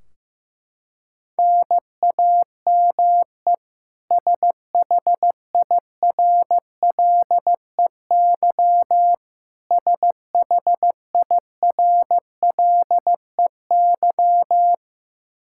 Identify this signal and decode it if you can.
Morse code, heard as a keyed tone, about 15 wpm, 700 Hz. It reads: NAME SHIRLEY SHIRLEY
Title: K1USN SST QSO Practice 15wpm